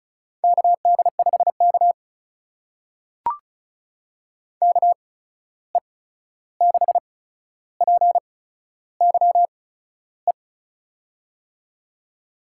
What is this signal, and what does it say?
Morse code, heard as a keyed tone, about 36 wpm, spaced as for 13 wpm, 700 Hz. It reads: KD5K KE6PYE
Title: USA Call Signs - 8x Character Spacing - 35wpm